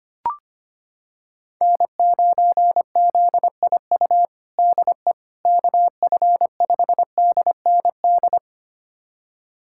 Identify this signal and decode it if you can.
Morse code, heard as a keyed tone, about 25 wpm, 700 Hz. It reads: N9ZIU DE KF5DND
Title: USA Callsign DE Callsign 25wpm